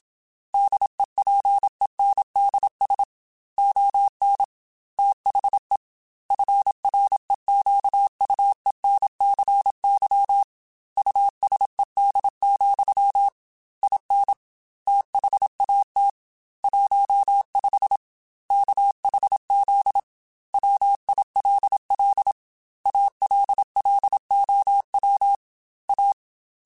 Morse code, heard as a keyed tone, about 27 wpm, 800 Hz. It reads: DEPENDS ON THE FREQUENCY USED, IN THAT 15 KHZ WILL ALLOW A
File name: Technical Subjects / Technical Subjects-mp3s / Sw Pwr Sup @25